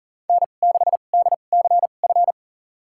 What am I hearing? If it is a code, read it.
Morse code, heard as a keyed tone, about 40 wpm, spaced as for 29 wpm, 700 Hz. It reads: N6DCF
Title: USA Call Signs - 5 Characters - 2x Character Spacing - 40wpm